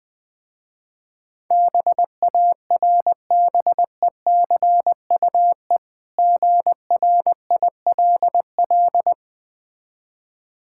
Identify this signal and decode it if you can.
Morse code, heard as a keyed tone, about 20 wpm, 700 Hz. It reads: BARBECUE GRILL
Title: Easy Words - Speed-Racing with Hints - 20wpm